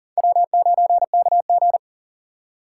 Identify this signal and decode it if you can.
Morse code, heard as a keyed tone, about 54 wpm, spaced as for 47 wpm, 700 Hz. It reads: W9KG
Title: Speed Racing - USA 1x2 Call Signs 40wpm